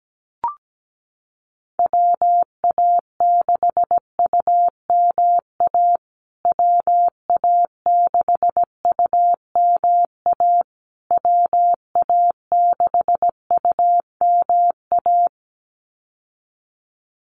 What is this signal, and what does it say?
Morse code, heard as a keyed tone, about 17 wpm, 700 Hz. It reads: WA6UMA WA6UMA WA6UMA
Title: USA Call Signs Repeated 3x 17wpm